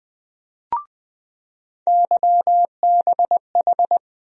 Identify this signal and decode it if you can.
Morse code, heard as a keyed tone, about 20 wpm, 700 Hz. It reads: YBH
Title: USA Call Sign Postfixes - 3 Letters 20wpm